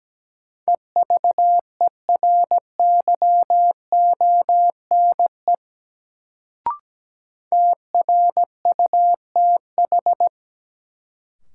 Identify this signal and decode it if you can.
Morse code, heard as a keyed tone, about 17 wpm, 700 Hz. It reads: EVERYONE TRUTH
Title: Sentences from Top 800 Words - New Word Drill - 17wpm